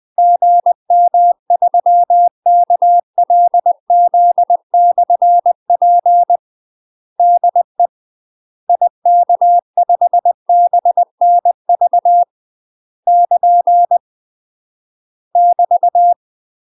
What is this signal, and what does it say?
Morse code, heard as a keyed tone, about 20 wpm, 700 Hz. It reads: GM3KLZ/P DE IK5BNV <KN> <BT>
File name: UK MASTER QSOs TRAINING/20WPM